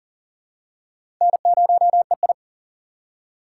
Morse code, heard as a keyed tone, about 40 wpm, 700 Hz. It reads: N0EI
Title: USA Call Signs - 4 Characters - 4x Character Spacing - 40wpm